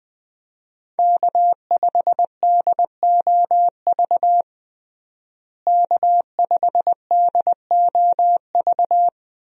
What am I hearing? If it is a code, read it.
Morse code, heard as a keyed tone, about 21 wpm, 700 Hz. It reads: K5DOV K5DOV
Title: USA Call Signs - 5 Characters - Repeated 3x - 20wpm